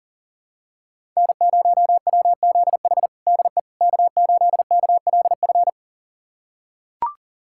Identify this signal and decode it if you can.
Morse code, heard as a keyed tone, about 40 wpm, 700 Hz. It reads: N0WZH DE K8KLF
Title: USA Callsign DE Callsign - 5 Characters - 40wpm